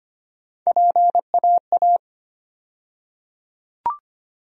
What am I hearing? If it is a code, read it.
Morse code, heard as a keyed tone, about 25 wpm, 700 Hz. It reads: PAA
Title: USA Call Sign Postfixes - 3 Letters 25wpm